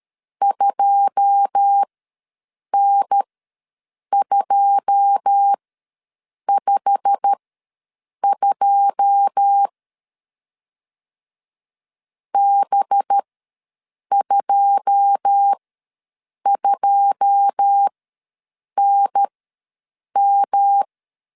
Morse code, heard as a keyed tone, about 13 wpm, spaced as for 7 wpm, 800 Hz. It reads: 2N252 B22NM
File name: LEZIONI / 9 2 ws 12wpm 3406